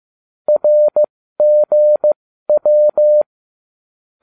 Morse code, heard as a keyed tone, about 15 wpm, spaced as for 13 wpm, 600 Hz. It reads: RGW